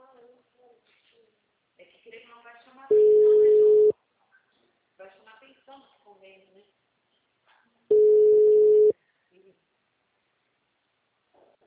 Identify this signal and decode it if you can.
Ringback tone (Europe, 425 Hz), with speech.